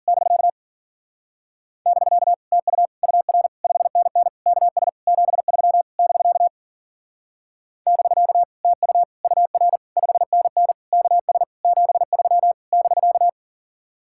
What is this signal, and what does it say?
Morse code, heard as a keyed tone, about 50 wpm, 700 Hz. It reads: <BK> <BK> TU UR 5NN KS 73 <BK> <BK> TU UR 5NN KS 73 <BK>
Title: Speed-Racing Common QSO Phrases Encore - 40wpm